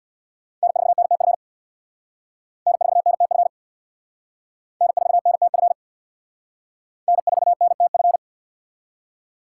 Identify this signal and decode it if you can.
Morse code, heard as a keyed tone, about 61 wpm, 700 Hz. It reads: N4NTF N4NTF N4NTF N4NTF
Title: Speed Racing - USA 1x3 Call Signs 50wpm